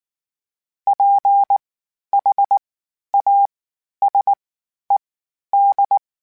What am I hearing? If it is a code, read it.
Morse code, heard as a keyed tone, about 19 wpm, spaced as for 11 wpm, 800 Hz. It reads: PHASED
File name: (spaced) / FromSMs spaced mp3s / Antennas sp@18